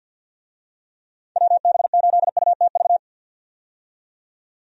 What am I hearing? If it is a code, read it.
Morse code, heard as a keyed tone, about 52 wpm, 700 Hz. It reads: WB8UTV
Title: USA Call Signs - 6 Characters - 50wpm